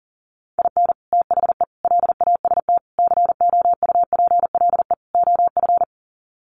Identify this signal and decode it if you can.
Morse code, heard as a keyed tone, about 40 wpm, 700 Hz. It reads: IN THE LAST COUPLE OF
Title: Sets of 5 Words 40wpm